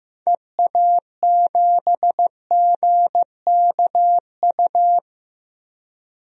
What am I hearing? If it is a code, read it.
Morse code, heard as a keyed tone, about 15 wpm, 700 Hz. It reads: EA7GKU
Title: DXCC-100 Call Signs - 6 Characters - 15wpm